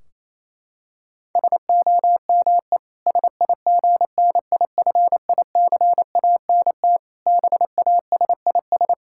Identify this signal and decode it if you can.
Morse code, heard as a keyed tone, about 28 wpm, 700 Hz. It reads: SOME SIGNIFICANT BASIS
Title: Sentences from Top 1600 Words 28wpm